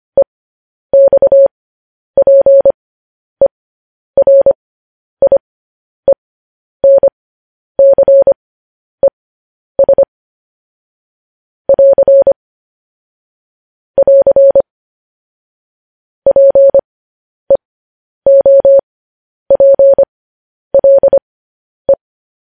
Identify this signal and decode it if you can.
Morse code, heard as a keyed tone, about 25 wpm, spaced as for 10 wpm, 550 Hz. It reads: EXPERIENCES <AR> <AR> PEOPLE
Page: news headlines in morse code at 10 wpm